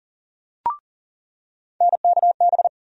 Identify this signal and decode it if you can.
Morse code, heard as a keyed tone, about 40 wpm, 700 Hz. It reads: NKB
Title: USA Call Sign Postfixes - 3 Letters 40wpm